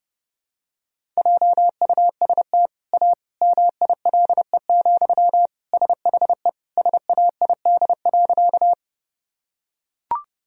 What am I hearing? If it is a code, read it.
Morse code, heard as a keyed tone, about 32 wpm, 700 Hz. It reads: JUST A MILE, SHE SAID.
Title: Sentences from Top 1000 Words - Review Part 1 - 30wpm